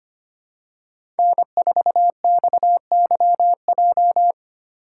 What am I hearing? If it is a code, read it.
Morse code, heard as a keyed tone, about 25 wpm, 700 Hz. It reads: N4XYJ
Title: USA Call Signs - 5 Characters - 3x Character Spacing - 25wpm